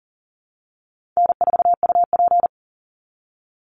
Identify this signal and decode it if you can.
Morse code, heard as a keyed tone, about 40 wpm, 700 Hz. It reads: N4UP
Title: Call Signs Repeated 2x - 40wpm